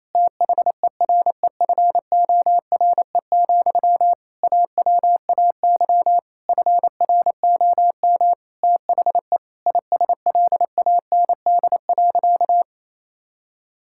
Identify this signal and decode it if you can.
Morse code, heard as a keyed tone, about 28 wpm, 700 Hz. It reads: THEREFORE, AWAY FROM THE ISLAND.
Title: Sentences from Top 1000 Words - Review Part 5 - 28wpm